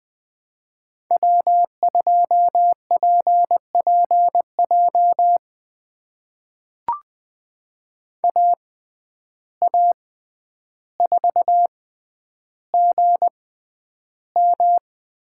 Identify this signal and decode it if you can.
Morse code, heard as a keyed tone, about 20 wpm, spaced as for 10 wpm, 700 Hz. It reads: W2PPJ AA4GM
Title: USA Call Signs - 5 Characters - 6x Character Spacing - 20wpm